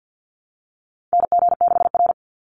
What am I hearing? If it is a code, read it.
Morse code, heard as a keyed tone, about 50 wpm, 700 Hz. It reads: NG6R
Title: NA SOTA Chaser Callsigns 50wpm